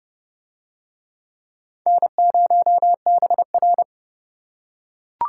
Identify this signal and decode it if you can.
Morse code, heard as a keyed tone, about 29 wpm, 700 Hz. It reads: N0BR